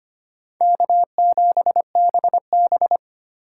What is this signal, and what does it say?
Morse code, heard as a keyed tone, about 31 wpm, 700 Hz. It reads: K7BB
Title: Speed Racing - USA 1x2 Call Signs 25wpm